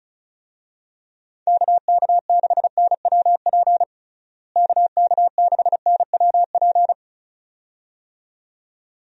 Morse code, heard as a keyed tone, about 34 wpm, 700 Hz. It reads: KK6NWP KK6NWP
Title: USA Call Signs - 6 Characters - Repeated 2x - 35wpm